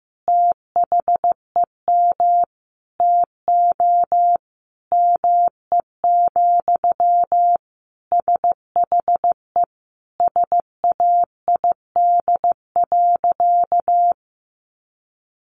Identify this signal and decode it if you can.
Morse code, heard as a keyed tone, about 15 wpm, 700 Hz. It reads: THEM TO ME, SHE SAID.
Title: Sentences from Top 500 Words 15wpm